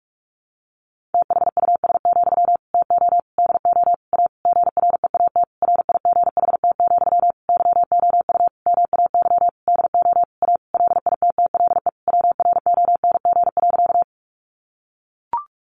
Morse code, heard as a keyed tone, about 44 wpm, 700 Hz. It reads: THUS, TO DO A GREAT RIGHT, YOU MAY DO A LITTLE WRONG.
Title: Sentences from Top 900 Words 45wpm